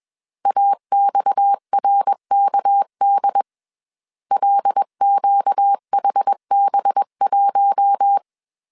Morse code, heard as a keyed tone, about 21 wpm, 800 Hz. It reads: AXRKD LQHB1
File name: CW / DET / 1 6 1500C S B 20wpm 2048